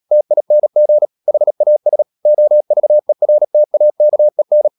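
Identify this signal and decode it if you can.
Morse code, heard as a keyed tone, about 37 wpm, 600 Hz. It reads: TING HAS OVERTAKEN
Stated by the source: Morse code practice at 35–39 wpm sending English text of Ken Follett's novel Never, from CW med Gnister.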